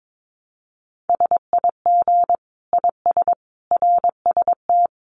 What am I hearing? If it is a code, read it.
Morse code, heard as a keyed tone, about 22 wpm, 700 Hz. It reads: SIG IS RST
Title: Common QSO Phrases 22wpm